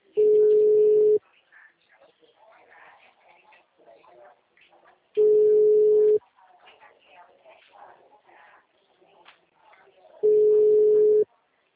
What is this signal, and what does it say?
Ringback tone (Europe, 425 Hz), with speech.
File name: out- -2018- - - .76546